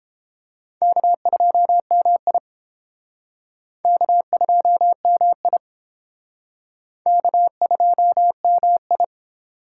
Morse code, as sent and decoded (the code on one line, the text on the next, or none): -.- ..--- -- .. / -.- ..--- -- .. / -.- ..--- -- ..
K2MI K2MI K2MI